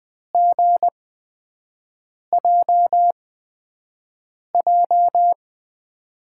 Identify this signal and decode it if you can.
Morse code, heard as a keyed tone, about 21 wpm, spaced as for 5 wpm, 700 Hz. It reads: GJJ
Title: USA Call Signs - 6 Characters - 8x Character Spacing - 20wpm